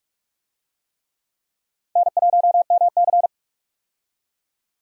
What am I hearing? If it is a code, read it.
Morse code, heard as a keyed tone, about 46 wpm, 700 Hz. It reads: N1MC
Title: USA Call Signs - 4 Characters - 45wpm